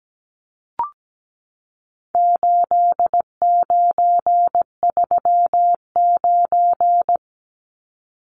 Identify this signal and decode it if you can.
Morse code, heard as a keyed tone, about 17 wpm, 700 Hz. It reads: 8939
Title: Numbers - 4-Digit 17wpm